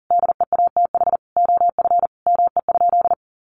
Morse code, heard as a keyed tone, about 37 wpm, 700 Hz. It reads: DEATH OF ME?
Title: Sentences from Top 1100 Words 40wpm